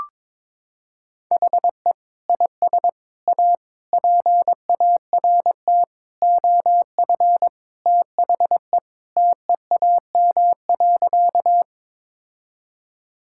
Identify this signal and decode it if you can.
Morse code, heard as a keyed tone, about 22 wpm, 700 Hz. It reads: HE IS A PART OF THE TEAM.
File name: practice / 22wpm / Sentences with 4-Word Phrases - 22wpm